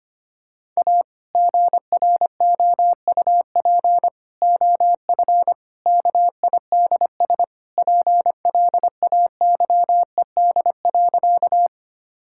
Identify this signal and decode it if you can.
Morse code, heard as a keyed tone, about 25 wpm, 700 Hz. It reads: A GROUP OF KIDS PLAYED.
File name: practice / 25wpm / Sentences with 3-Word Phrases - 25wpm